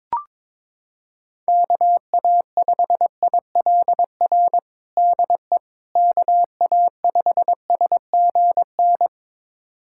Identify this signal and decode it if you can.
Morse code, heard as a keyed tone, about 22 wpm, 700 Hz. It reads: KA5ILR DE KA5SGN
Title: USA Callsign DE Callsign - 6 Characters - 22wpm